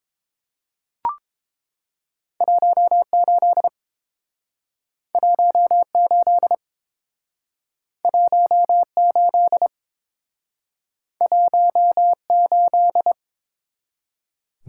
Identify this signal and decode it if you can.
Morse code, heard as a keyed tone, about 29 wpm, 700 Hz. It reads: 18 18 18 18